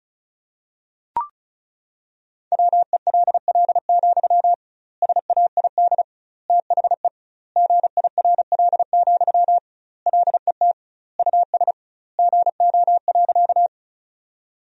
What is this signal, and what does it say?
Morse code, heard as a keyed tone, about 36 wpm, 700 Hz. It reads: WELL, SAID THE GIRL, LET US GO.
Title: 2x Word-Spacing - Sentences from Top 400 Words - 35wpm